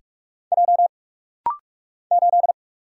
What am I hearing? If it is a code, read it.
Morse code, heard as a keyed tone, about 46 wpm, 700 Hz. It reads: J 8
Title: Single Letter-Number Alt - Rapid-Fire 45wpm